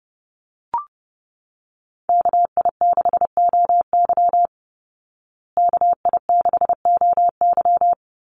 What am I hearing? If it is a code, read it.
Morse code, heard as a keyed tone, about 30 wpm, 700 Hz. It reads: KI6OY KI6OY
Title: Call Signs Repeated 3x - 30wpm